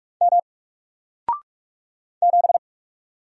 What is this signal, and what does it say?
Morse code, heard as a keyed tone, about 44 wpm, 700 Hz. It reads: M 7
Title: Single Letter-Number Alt 45wpm